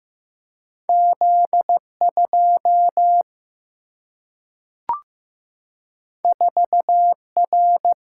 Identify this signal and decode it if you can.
Morse code, heard as a keyed tone, about 15 wpm, 700 Hz. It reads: Z 2 4 R
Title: Single-Letter Single-Number 15wpm